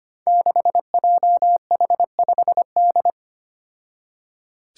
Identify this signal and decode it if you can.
Morse code, heard as a keyed tone, about 25 wpm, 700 Hz. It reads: 6JH5D
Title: Five Challenging Letter-Number Combinations 25wpm